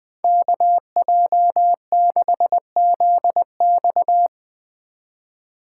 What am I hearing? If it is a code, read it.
Morse code, heard as a keyed tone, about 20 wpm, 700 Hz. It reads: KJ6ZX
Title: Speed Racing - USA 2x2 Call Signs 17wpm